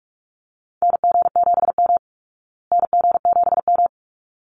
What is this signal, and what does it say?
Morse code, heard as a keyed tone, about 44 wpm, 700 Hz. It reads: NG7M NG7M
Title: Call Signs Repeated 3x - 45wpm